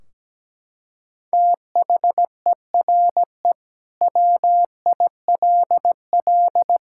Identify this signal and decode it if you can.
Morse code, heard as a keyed tone, about 17 wpm, 700 Hz. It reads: THERE WILL
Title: Sentences from Top 1600 Words 17wpm